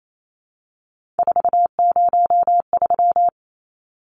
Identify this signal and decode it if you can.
Morse code, heard as a keyed tone, about 28 wpm, 700 Hz. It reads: 403
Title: Numbers - 3-Digit 28wpm